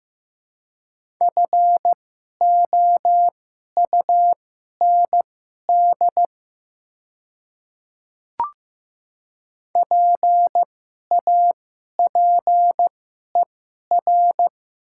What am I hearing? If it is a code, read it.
Morse code, heard as a keyed tone, about 15 wpm, spaced as for 11 wpm, 700 Hz. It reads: FOUND PAPER
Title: 2x Character Spacing - Top 500 Words - 15wpm